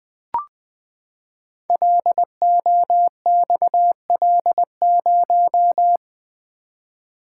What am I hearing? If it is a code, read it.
Morse code, heard as a keyed tone, about 20 wpm, 700 Hz. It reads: LOXL0
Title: Code Groups - 5 Letter:Numbers - 20wpm